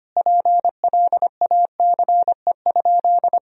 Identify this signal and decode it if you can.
Morse code, heard as a keyed tone, about 25 wpm, 700 Hz. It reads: PLACE?
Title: Sentences from Top 200 Words and Abbreviations 25wpm